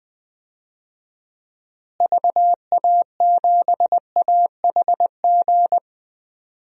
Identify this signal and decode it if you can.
Morse code, heard as a keyed tone, about 20 wpm, 700 Hz. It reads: VA7AHG
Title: DXCC-100 Call Signs - 6 Characters - Repeated 2x - 20wpm